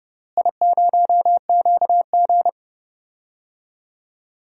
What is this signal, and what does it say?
Morse code, heard as a keyed tone, about 30 wpm, 700 Hz. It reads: I0QG
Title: DXCC-100 Call Signs - 4 Characters - Repeated 2x - 30wpm